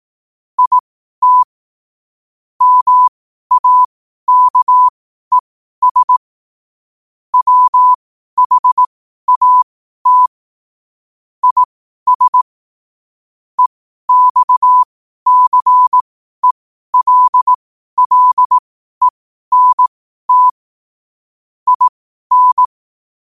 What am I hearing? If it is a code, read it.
Morse code, heard as a keyed tone, about 18 wpm, spaced as for 12 wpm, 1000 Hz. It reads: IT MAKES WHAT IS EXCELLENT IN
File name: qotdfiles / 12wpm